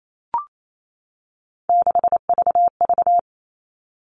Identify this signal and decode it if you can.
Morse code, heard as a keyed tone, about 28 wpm, 700 Hz. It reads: 6VV
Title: All Cut Numbers - 3-Digit 28wpm